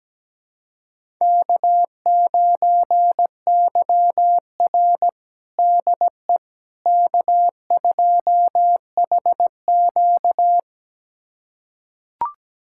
Morse code, heard as a keyed tone, about 17 wpm, 700 Hz. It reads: K9YR DE K2HQ
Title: USA Callsign DE Callsign - 4 Characters - 17wpm